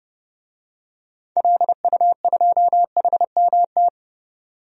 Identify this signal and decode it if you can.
Morse code, heard as a keyed tone, about 32 wpm, 700 Hz. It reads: LU2HMT
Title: DXCC-100 Call Signs - 6 Characters - 30wpm